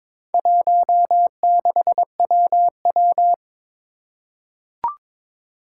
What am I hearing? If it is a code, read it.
Morse code, heard as a keyed tone, about 22 wpm, 700 Hz. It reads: 16WW
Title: Four Challenging Letter-Number Combinations 22wpm